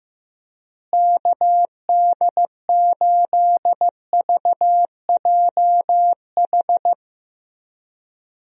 Morse code, heard as a keyed tone, about 15 wpm, 700 Hz. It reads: KD8VJH
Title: USA Call Signs - 6 Characters - Repeated 2x - 15wpm